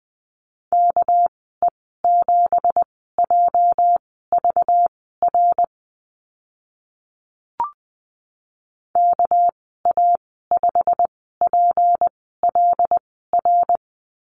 Morse code, heard as a keyed tone, about 20 wpm, spaced as for 14 wpm, 700 Hz. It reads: KE7JVR KA5PLR
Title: 213 Lesson 34g - Number 8 - Call Signs 20wpm